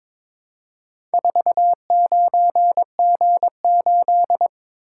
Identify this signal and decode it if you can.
Morse code, heard as a keyed tone, about 22 wpm, 700 Hz. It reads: 49G8
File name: practice / 22wpm / Four Letter-Number Combinations - 22wpm